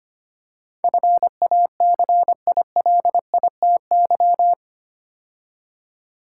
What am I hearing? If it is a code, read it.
Morse code, heard as a keyed tone, about 25 wpm, 690 Hz. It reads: FACILITY
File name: 8 Letter Words - First 3 Letters Given - 25wpm